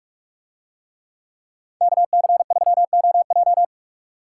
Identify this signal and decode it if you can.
Morse code, heard as a keyed tone, about 44 wpm, 700 Hz. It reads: KC3OJ